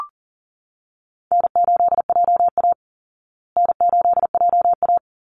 Morse code, heard as a keyed tone, about 40 wpm, 700 Hz. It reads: N8JA N8JA
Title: Easy Call Signs Repeated 2x - 40wpm